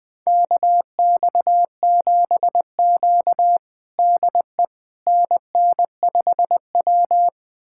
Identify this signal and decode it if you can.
Morse code, heard as a keyed tone, about 21 wpm, 700 Hz. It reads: KX7Q DE NN5W
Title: USA Callsign DE Callsign - 4 Characters - 20wpm